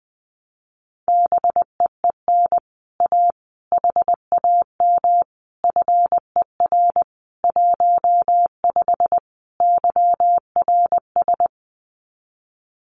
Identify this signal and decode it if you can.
Morse code, heard as a keyed tone, about 20 wpm, 700 Hz. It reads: BEEN A HAM FER 15 YRS
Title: Common QSO Phrases Encore 20wpm